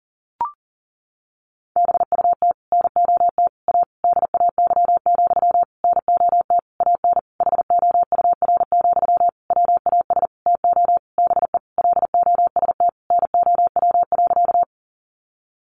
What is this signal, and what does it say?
Morse code, heard as a keyed tone, about 40 wpm, 700 Hz. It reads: BUT NOT A DAY, NOT AN HOUR, WAS TO BE LOST NOW.
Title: Sentences from Top 1400 Words 40wpm